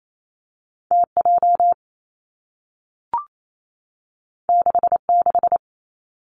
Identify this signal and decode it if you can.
Morse code, heard as a keyed tone, about 28 wpm, 700 Hz. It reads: T J 6 6
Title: Two Challenging Letter-Number Combinations 28wpm